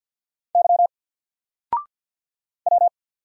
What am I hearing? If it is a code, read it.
Morse code, heard as a keyed tone, about 48 wpm, 700 Hz. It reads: Y W
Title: Single Character (Letter:Number:Punctuation:Prosign) 50wpm